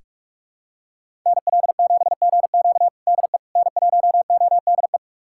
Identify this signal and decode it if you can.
Morse code, heard as a keyed tone, about 46 wpm, 700 Hz. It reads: NL7GQ DE N1ODE
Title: USA Callsign DE Callsign - 5 Characters - 45wpm